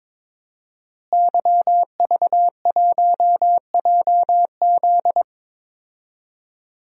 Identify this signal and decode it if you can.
Morse code, heard as a keyed tone, about 22 wpm, 700 Hz. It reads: YV1JZ
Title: DXCC-100 Call Signs - 5 Characters - Repeated 3x - 22wpm